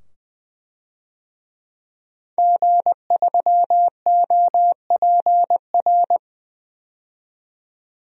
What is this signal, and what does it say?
Morse code, heard as a keyed tone, about 21 wpm, 700 Hz. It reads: G3OPR
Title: DXCC-100 Call Signs - 5 Characters - 20wpm